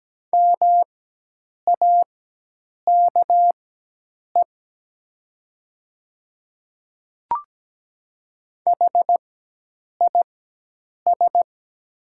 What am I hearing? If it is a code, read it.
Morse code, heard as a keyed tone, about 17 wpm, spaced as for 8 wpm, 700 Hz. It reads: MAKE HIS
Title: 4x Character Spacing - Top 200 Words - 17wpm